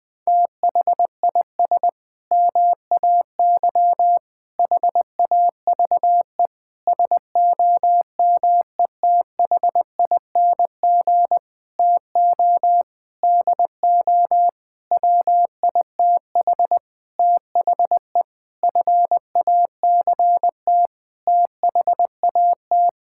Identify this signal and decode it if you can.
Morse code, heard as a keyed tone, about 20 wpm, 700 Hz. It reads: THIS MAY HAVE SOMETHING TO DO WITH THE FACT THAT
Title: Sentences with 8-Word Phrases - 20wpm